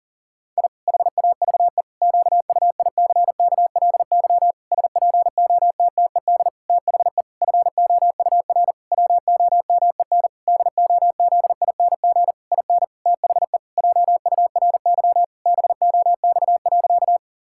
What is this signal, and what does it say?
Morse code, heard as a keyed tone, about 40 wpm, 700 Hz. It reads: I HAVE QUICKLY SPOTTED THE FOUR WOMEN DOZING IN THE JURY BOX.
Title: Pangram Challenge v2 - 40wpm